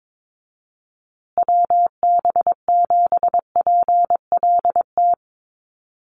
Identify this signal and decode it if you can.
Morse code, heard as a keyed tone, about 22 wpm, 700 Hz. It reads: WB7PLT
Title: USA Call Signs Repeated 3x 22wpm